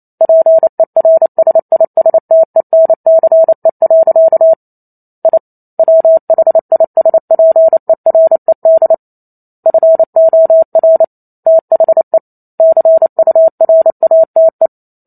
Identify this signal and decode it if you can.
Morse code, heard as a keyed tone, about 28 wpm, 650 Hz. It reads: PERSISTENCE. I WHISPERED FOR THE CURATE